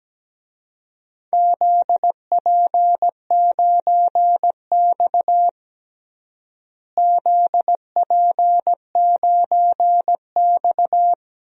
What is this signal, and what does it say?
Morse code, heard as a keyed tone, about 17 wpm, 700 Hz. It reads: ZP9X ZP9X
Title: DXCC-100 Call Signs - 4 Characters - Repeated 3x - 17wpm